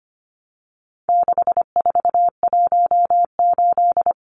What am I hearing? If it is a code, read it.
Morse code, heard as a keyed tone, about 25 wpm, 700 Hz. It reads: 6418
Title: Numbers - 4-Digit 25wpm